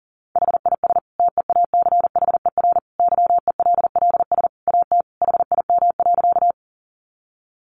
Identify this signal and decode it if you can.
Morse code, heard as a keyed tone, about 40 wpm, 700 Hz. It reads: HIS TEACHER YELLS AT HIM.
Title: Additional Easy Phrases 40wpm